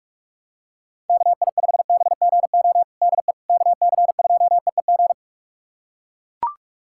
Morse code, heard as a keyed tone, about 44 wpm, 700 Hz. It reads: KI5BGO DE KC2EEG
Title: USA Callsign DE Callsign 45wpm